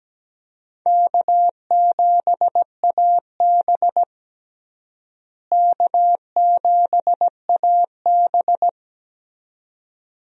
Morse code, heard as a keyed tone, about 17 wpm, 700 Hz. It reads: K7AB K7AB
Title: USA Call Signs - 4 Characters - Repeated 2x - 17wpm